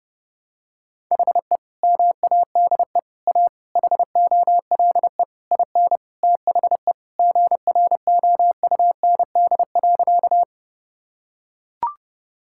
Morse code, heard as a keyed tone, about 30 wpm, 700 Hz. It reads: HE MADE A HOLE IN THE GROUND.
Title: Sentences from Top 1600 Words 30wpm